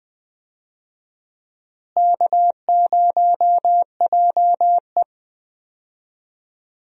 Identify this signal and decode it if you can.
Morse code, heard as a keyed tone, about 20 wpm, 700 Hz. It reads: K0JE
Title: USA Call Signs - 4 Characters - Repeated 2x - 20wpm